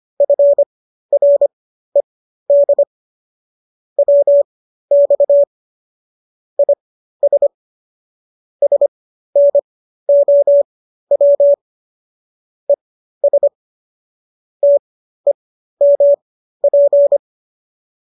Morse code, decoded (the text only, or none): FRED WX IS SNOW ES TEMP